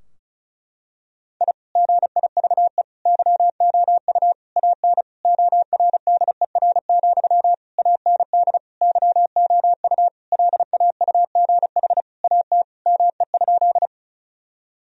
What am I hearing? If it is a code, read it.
Morse code, heard as a keyed tone, about 34 wpm, 700 Hz. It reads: I GIVE YOU AN ORDER, AND YOU LAUGH AT ME?
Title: Sentences from Top 1000 Words - Review Part 2 - 35wpm